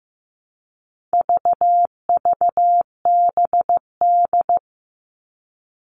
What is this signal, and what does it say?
Morse code, heard as a keyed tone, about 15 wpm, 700 Hz. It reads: VVBD
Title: All Cut Numbers - 4-Digit 15wpm